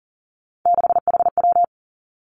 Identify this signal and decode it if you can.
Morse code, heard as a keyed tone, about 43 wpm, 700 Hz. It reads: 6HW
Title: Three Challenging Letter-Number Combinations 40wpm